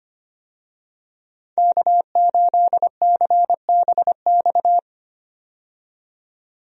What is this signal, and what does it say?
Morse code, heard as a keyed tone, about 25 wpm, 700 Hz. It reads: K8CBX
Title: USA Call Signs - 5 Characters - 25wpm